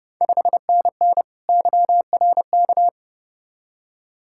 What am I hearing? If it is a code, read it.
Morse code, heard as a keyed tone, about 30 wpm, 700 Hz. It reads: 5NN YRK
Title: Ontario QSO Party Exchange 30wpm